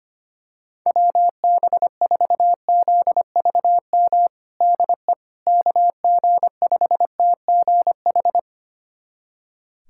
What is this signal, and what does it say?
Morse code, heard as a keyed tone, about 25 wpm, 700 Hz. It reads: WB4ZVM DE KG5TGH
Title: USA Callsign DE Callsign - 6 Characters - 25wpm